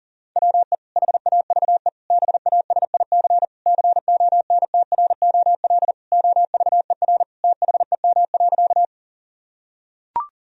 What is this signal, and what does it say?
Morse code, heard as a keyed tone, about 40 wpm, 700 Hz. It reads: WE HAVE BASIC CONTROL OVER THEM.
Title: Sentences from Top 1000 Words - Review Part 3 - 40wpm